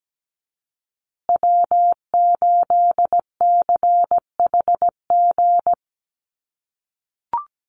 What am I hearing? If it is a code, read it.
Morse code, heard as a keyed tone, about 17 wpm, 700 Hz. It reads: W8CHG
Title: USA Call Signs Repeated 3x 17wpm